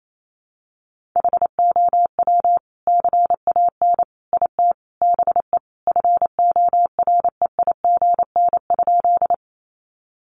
Morse code, heard as a keyed tone, about 28 wpm, 700 Hz. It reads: HOW CAN IT BE FOREIGN?
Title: Sentences from Top 1400 Words 28wpm